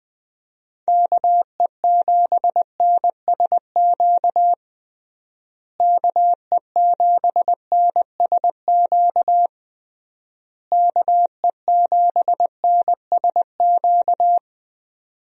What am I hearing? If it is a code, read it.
Morse code, heard as a keyed tone, about 20 wpm, 700 Hz. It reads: KE7NSQ KE7NSQ KE7NSQ
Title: USA Call Signs - 6 Characters - Repeated 3x - 20wpm